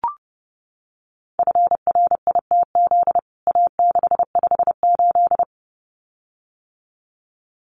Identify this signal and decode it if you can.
Morse code, heard as a keyed tone, about 30 wpm, 700 Hz. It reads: FRITZ A658
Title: CWT Exchange 30wpm